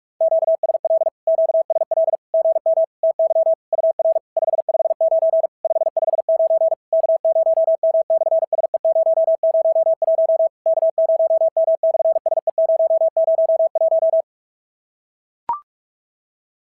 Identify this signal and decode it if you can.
Morse code, heard as a keyed tone, about 46 wpm, 650 Hz. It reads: QSL QSL GM TY UR 559 559 K0M/SE001 K0M/SE001
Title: SOTA QSO Practice 45wpm